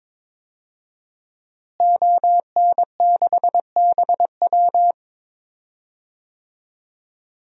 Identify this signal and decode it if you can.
Morse code, heard as a keyed tone, about 22 wpm, 700 Hz. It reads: ON6BW
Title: DXCC-100 Call Signs - 5 Characters - Repeated 2x - 22wpm